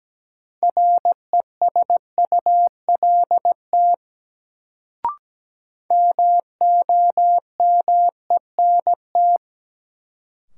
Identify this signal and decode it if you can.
Morse code, heard as a keyed tone, about 17 wpm, 700 Hz. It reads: RESULT MOMENT
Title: Sets of 3 Words - New Word Drill - 17wpm